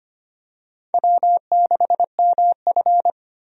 Morse code, heard as a keyed tone, about 25 wpm, 700 Hz. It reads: W6MF
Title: USA Call Signs - 4 Characters - 25wpm